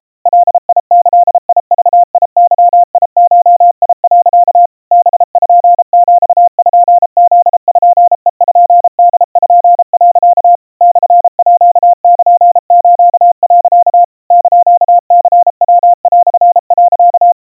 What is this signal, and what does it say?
Morse code (Japanese Wabun code), heard as a keyed tone, about 34 wpm, 700 Hz. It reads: ガギグゲゴ、 パピプペポ、 モールス、 <KN>ニヤ)、
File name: iroha 110 cpm 700 Hz Crispy